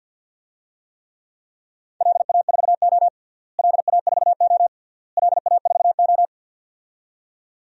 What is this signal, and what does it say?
Morse code, heard as a keyed tone, about 50 wpm, 700 Hz. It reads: LA4O LA4O LA4O
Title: DXCC-100 Call Signs - 4 Characters - Repeated 3x - 50wpm